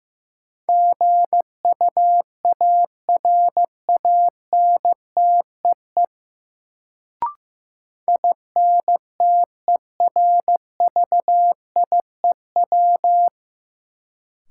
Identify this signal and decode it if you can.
Morse code, heard as a keyed tone, about 16 wpm, 700 Hz. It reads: GUARANTEE INTERVIEW
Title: Sentences from Top 1600 Words - New Word Drill - 15wpm